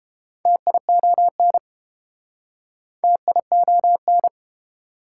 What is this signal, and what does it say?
Morse code, heard as a keyed tone, about 27 wpm, 700 Hz. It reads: TION TION
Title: Speed-Racing Prefixes and Suffixes 22wpm